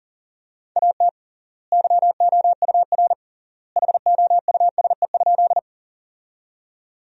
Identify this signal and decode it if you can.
Morse code, heard as a keyed tone, about 40 wpm, 700 Hz. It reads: AT YOUR HOUSE?
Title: 3x Word-Spacing - Sentences from Top 200 Words - 40wpm